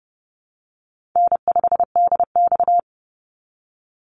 Morse code, heard as a keyed tone, about 30 wpm, 700 Hz. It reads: N5DX
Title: Easy Call Signs Repeated 2x - 30wpm